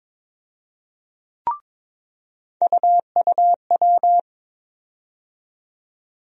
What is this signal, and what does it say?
Morse code, heard as a keyed tone, about 22 wpm, 700 Hz. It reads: UUW
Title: USA Call Sign Postfixes - 3 Letters 22wpm